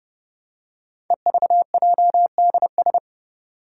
Morse code, heard as a keyed tone, about 28 wpm, 700 Hz. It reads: EVJDS
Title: Code Groups - 5 Letters - 30wpm